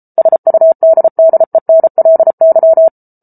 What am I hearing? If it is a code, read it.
Morse code, heard as a keyed tone, about 34 wpm, 650 Hz. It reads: SUDDENLY